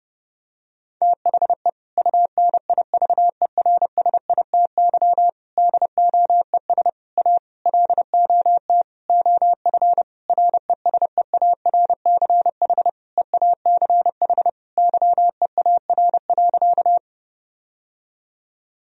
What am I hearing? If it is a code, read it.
Morse code, heard as a keyed tone, about 30 wpm, 700 Hz. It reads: THE UNIVERSITY DOES A LOT OF RESEARCH EACH YEAR.
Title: Sentences from Top 1000 Words - Review Part 4 - 30wpm